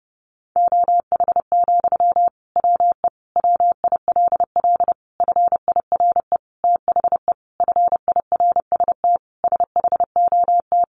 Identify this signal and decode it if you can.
Morse code, heard as a keyed tone, about 30 wpm, 700 Hz. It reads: OH, WE WILL FIRE THE FIRST SHOT
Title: Sentences from Top 1100 Words 30wpm